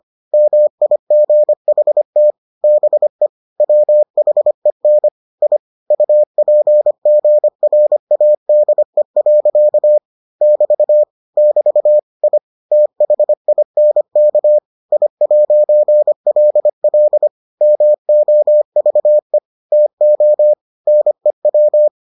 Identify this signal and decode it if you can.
Morse code, heard as a keyed tone, about 25 wpm, 600 Hz. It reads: MIGHT BE WHEN I UPGRADE. <BT> <BT> I THINK I'LL MOVE TO NEW